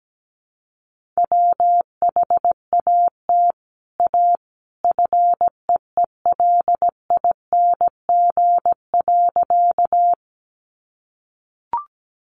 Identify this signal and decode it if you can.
Morse code, heard as a keyed tone, about 17 wpm, 700 Hz. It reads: WHAT A FEELING.
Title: Sentences from Top 1200 Words 17wpm